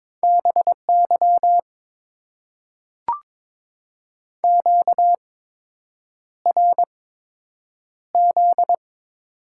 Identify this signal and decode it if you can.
Morse code, heard as a keyed tone, about 21 wpm, spaced as for 8 wpm, 700 Hz. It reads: BY QRZ
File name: Top 90 QSO Elements - 8x Character Spacing - 22wpm